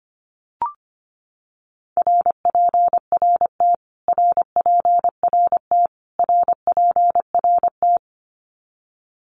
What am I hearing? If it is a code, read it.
Morse code, heard as a keyed tone, about 25 wpm, 700 Hz. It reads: RPRT RPRT RPRT
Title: Abbreviations Repeated 3x - 25wpm